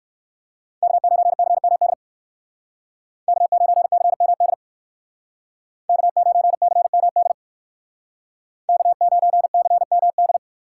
Morse code, heard as a keyed tone, about 59 wpm, 700 Hz. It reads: K9CMD K9CMD K9CMD K9CMD